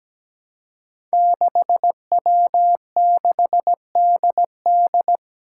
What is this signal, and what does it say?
Morse code, heard as a keyed tone, about 17 wpm, 700 Hz. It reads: 6W6DD
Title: Five Challenging Letter-Number Combinations 17wpm